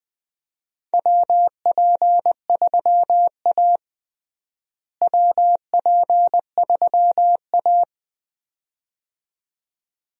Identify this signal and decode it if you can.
Morse code, heard as a keyed tone, about 20 wpm, 700 Hz. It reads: WP3A WP3A
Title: USA Call Signs - 4 Characters - Repeated 2x - 20wpm